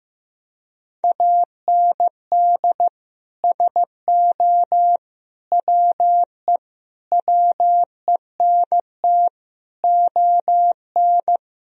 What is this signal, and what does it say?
Morse code, heard as a keyed tone, about 16 wpm, 700 Hz. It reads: AND SO WE WENT ON
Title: Sentences from Top 1000 Words - Review Part 1 - 15wpm